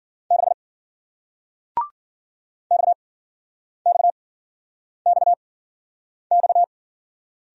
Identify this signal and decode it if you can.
Morse code, heard as a keyed tone, about 48 wpm, spaced as for 9 wpm, 700 Hz. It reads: 6 XXXX
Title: Speed-Racing Single Letter-Number 40wpm